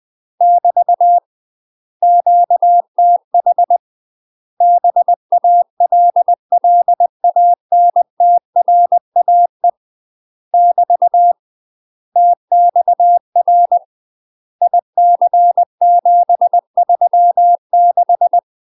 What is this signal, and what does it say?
Morse code, heard as a keyed tone, about 20 wpm, 700 Hz. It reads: <BT> QTH BALLANTRAE <BT> TXR IC736